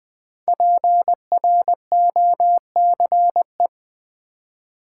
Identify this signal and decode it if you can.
Morse code, heard as a keyed tone, about 20 wpm, 700 Hz. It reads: PROCE
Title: Build A Word - 5 Letters At A Time - 10-Letter Words - 20wpm